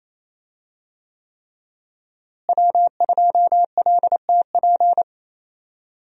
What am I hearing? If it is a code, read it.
Morse code, heard as a keyed tone, about 27 wpm, 700 Hz. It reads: W2LTP
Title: USA Call Signs - 5 Characters - Repeated 3x - 28wpm